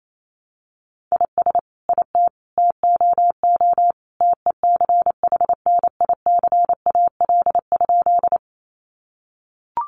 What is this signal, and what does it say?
Morse code, heard as a keyed tone, about 28 wpm, 700 Hz. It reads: IS IT TOO TECHNICAL?
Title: Sentences from Top 1100 Words 28wpm